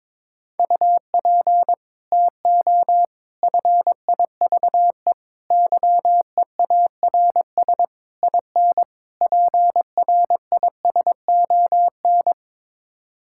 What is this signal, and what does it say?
Morse code, heard as a keyed tone, about 22 wpm, 700 Hz. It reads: UP TO FIVE YEARS IN PRISON
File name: practice / 22wpm / Sets of 6 Words - 22wpm